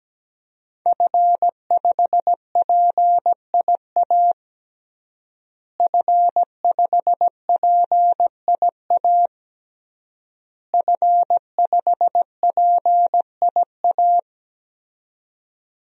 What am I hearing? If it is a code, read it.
Morse code, heard as a keyed tone, about 17 wpm, 700 Hz. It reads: F5PIA F5PIA F5PIA
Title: DXCC-100 Call Signs - 5 Characters - Repeated 3x - 17wpm